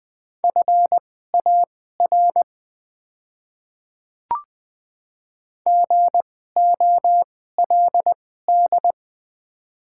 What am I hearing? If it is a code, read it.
Morse code, heard as a keyed tone, about 20 wpm, spaced as for 14 wpm, 700 Hz. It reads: FAR GOLD
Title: 2x Character Spacing - Top 500 Words - 20wpm